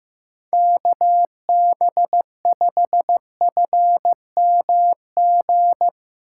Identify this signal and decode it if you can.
Morse code, heard as a keyed tone, about 15 wpm, 700 Hz. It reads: KB5FMG
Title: USA Call Signs - 6x Character Spacing - 15wpm